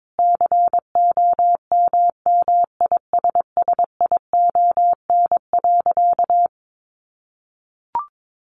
Morse code, heard as a keyed tone, about 22 wpm, 700 Hz. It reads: COMMISSION.
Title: Sentences from Top 1400 Words 22wpm